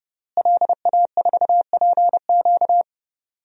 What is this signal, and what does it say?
Morse code, heard as a keyed tone, about 30 wpm, 700 Hz. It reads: LA4PQ
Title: DXCC-100 Call Signs - 5 Characters - Repeated 2x - 30wpm